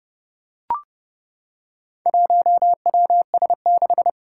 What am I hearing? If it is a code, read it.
Morse code, heard as a keyed tone, about 30 wpm, 700 Hz. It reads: 1WS6
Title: Four Challenging Letter-Number Combinations 30wpm